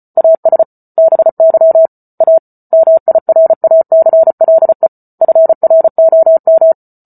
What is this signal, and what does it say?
Morse code, heard as a keyed tone, about 34 wpm, 650 Hz. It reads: AS BY A MIRACLE FROM